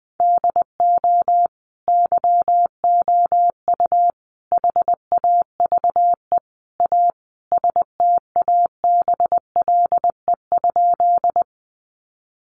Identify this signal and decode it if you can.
Morse code, heard as a keyed tone, about 20 wpm, 700 Hz. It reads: DO YOU HAVE A STABLE?
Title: Sentences from Top 1500 Words 20wpm